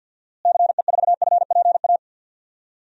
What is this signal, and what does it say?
Morse code, heard as a keyed tone, about 53 wpm, 700 Hz. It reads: CE4FPA
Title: DXCC-100 Call Signs - 6 Characters - Repeated 2x - 50wpm